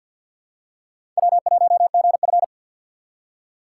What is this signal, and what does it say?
Morse code, heard as a keyed tone, about 53 wpm, 700 Hz. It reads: W1GF
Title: USA Call Signs - 4 Characters - 2x Character Spacing - 50wpm